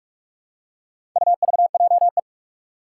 Morse code, heard as a keyed tone, about 41 wpm, 700 Hz. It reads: UVJE
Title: Four Challenging Letter-Number Combinations 45wpm